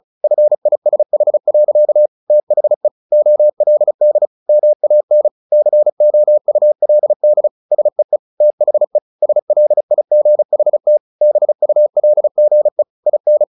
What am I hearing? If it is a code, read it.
Morse code, heard as a keyed tone, about 34 wpm, 600 Hz. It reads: FISH. THE OLD MAN COULD SEE THE SLIGHT BULGE IN
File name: the old man and the sea / the old man and the sea 1 35wpm